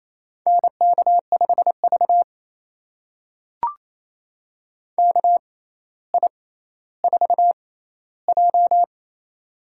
Morse code, heard as a keyed tone, about 28 wpm, spaced as for 14 wpm, 700 Hz. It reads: NK5V KI4J
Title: USA Call Signs - 4 Characters - 6x Character Spacing - 28wpm